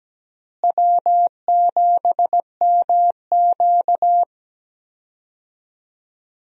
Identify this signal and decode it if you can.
Morse code, heard as a keyed tone, about 17 wpm, 700 Hz. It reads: W7MQ
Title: USA Call Signs - 4 Characters - Repeated 3x - 17wpm